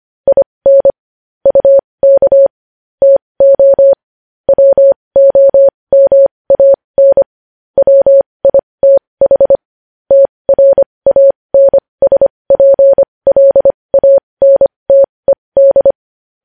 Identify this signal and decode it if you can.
Morse code, heard as a keyed tone, about 25 wpm, spaced as for 20 wpm, 550 Hz. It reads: IN UK TO WOMAN WITH TRANSPLANTED
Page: news headlines in morse code at 20 wpm